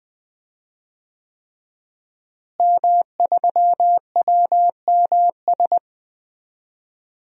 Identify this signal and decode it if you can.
Morse code, heard as a keyed tone, about 20 wpm, 700 Hz. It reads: M3WMS
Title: DXCC-100 Call Signs - 5 Characters - Repeated 3x - 20wpm